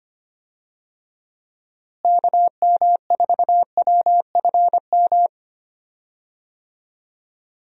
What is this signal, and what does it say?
Morse code, heard as a keyed tone, about 25 wpm, 700 Hz. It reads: KM4WFM
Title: USA Call Signs - 6 Characters - Repeated 3x - 25wpm